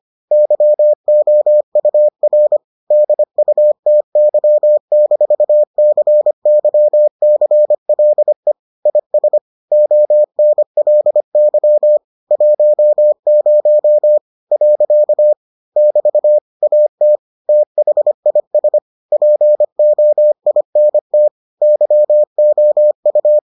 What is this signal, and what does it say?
Morse code, heard as a keyed tone, about 25 wpm, 600 Hz. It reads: YOUR DUTY-CYCLE IS ONLY 10 . <BT> AT THIS POINT YOU